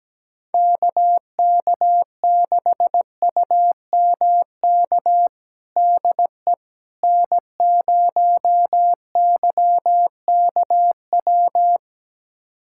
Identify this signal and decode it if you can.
Morse code, heard as a keyed tone, about 17 wpm, 700 Hz. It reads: KK6UMK DE N0YKW
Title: USA Callsign DE Callsign 17wpm